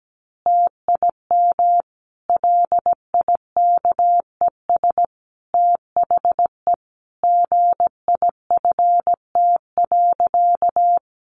Morse code, heard as a keyed tone, about 17 wpm, 700 Hz. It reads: TIM LIKES THE GIFT.
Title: Easy Phrases 17wpm